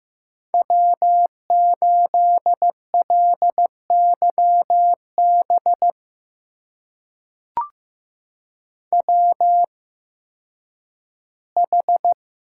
Morse code, heard as a keyed tone, about 15 wpm, spaced as for 10 wpm, 700 Hz. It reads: W8LYB WH
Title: USA Call Signs - 5 Characters - 8x Character Spacing - 15wpm